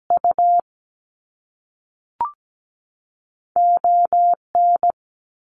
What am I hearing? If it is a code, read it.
Morse code, heard as a keyed tone, about 17 wpm, spaced as for 5 wpm, 700 Hz. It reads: UON
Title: Canadian Provinces and Territories 17wpm